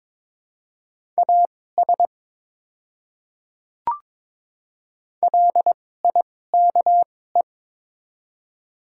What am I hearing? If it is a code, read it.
Morse code, heard as a keyed tone, about 22 wpm, spaced as for 16 wpm, 700 Hz. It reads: AS LIKE
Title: 2x Character Spacing - Top 100 Words - 22wpm